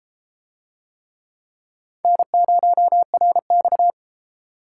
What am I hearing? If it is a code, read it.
Morse code, heard as a keyed tone, about 30 wpm, 700 Hz. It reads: N0RX